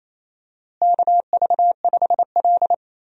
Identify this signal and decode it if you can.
Morse code, heard as a keyed tone, about 28 wpm, 700 Hz. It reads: KV5L
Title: USA Call Signs - 4 Characters - Repeated 3x - 28wpm